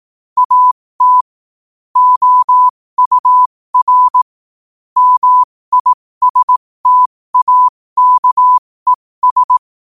Morse code, heard as a keyed tone, about 18 wpm, spaced as for 15 wpm, 1000 Hz. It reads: AT OUR MISTAKES
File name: 2024-11-18-15wpm